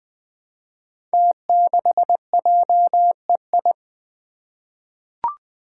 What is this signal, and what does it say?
Morse code, heard as a keyed tone, about 20 wpm, 700 Hz. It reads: T6JEI
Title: Five Challenging Letter-Number Combinations 20wpm